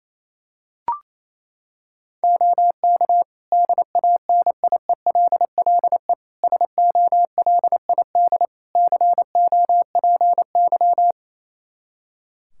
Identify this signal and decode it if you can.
Morse code, heard as a keyed tone, about 28 wpm, 700 Hz. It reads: OK DANIELLE SOLID COPY
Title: Practice QSO - Exchange 3 - More Info - 28wpm